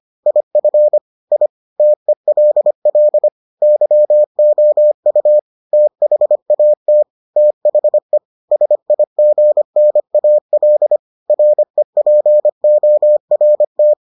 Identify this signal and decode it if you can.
Morse code, heard as a keyed tone, about 25 wpm, 600 Hz. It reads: IF I TELL YOU THAT THE SIGNAL REPORT